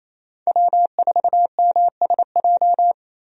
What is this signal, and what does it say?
Morse code, heard as a keyed tone, about 27 wpm, 700 Hz. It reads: W4MSJ
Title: USA Call Signs - 5 Characters - Repeated 2x - 28wpm